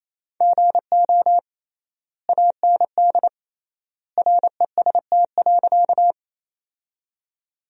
Morse code, heard as a keyed tone, about 28 wpm, 700 Hz. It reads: GO AND REST.
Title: 3x Word-Spacing - Sentences from Top 500 Words - 28wpm